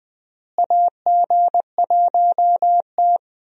Morse code, heard as a keyed tone, about 20 wpm, 700 Hz. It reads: AG1T
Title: USA Call Signs - 4 Characters - Repeated 3x - 20wpm